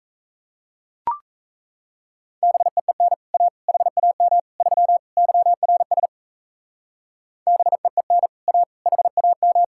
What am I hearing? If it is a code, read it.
Morse code, heard as a keyed tone, about 33 wpm, 700 Hz. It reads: BEEN A HAM 3 YRS BEEN A HAM
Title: Speed-Racing Common QSO Phrases - 28wpm